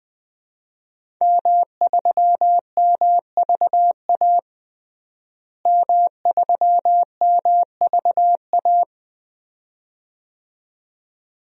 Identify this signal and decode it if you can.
Morse code, heard as a keyed tone, about 20 wpm, 700 Hz. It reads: M3MVA M3MVA
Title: DXCC-100 Call Signs - 5 Characters - Repeated 2x - 20wpm